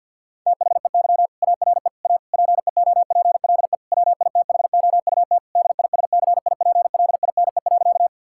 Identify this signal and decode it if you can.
Morse code, heard as a keyed tone, about 50 wpm, 700 Hz. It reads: THEY ARE A PEOPLE WITHOUT DISCIPLINE.
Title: Sentences from Top 1000 Words - Review Part 6 - 50wpm